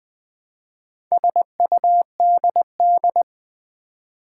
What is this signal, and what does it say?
Morse code, heard as a keyed tone, about 20 wpm, 690 Hz. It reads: SUDD
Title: Build A Word - 2 Letters At A Time - 6-Letter Words - 20wpm